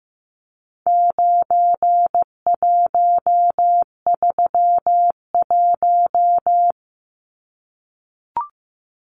Morse code, heard as a keyed tone, about 15 wpm, 700 Hz. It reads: 9131
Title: Numbers - 4-Digit 15wpm